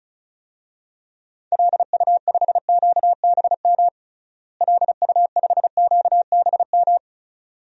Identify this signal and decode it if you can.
Morse code, heard as a keyed tone, about 36 wpm, 700 Hz. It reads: LU5QBM LU5QBM
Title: DXCC-100 Call Signs - 6 Characters - Repeated 2x - 35wpm